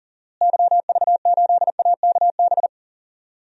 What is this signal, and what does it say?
Morse code, heard as a keyed tone, about 40 wpm, 700 Hz. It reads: YV8AKB
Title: DXCC-100 Call Signs - 6 Characters - Repeated 3x - 40wpm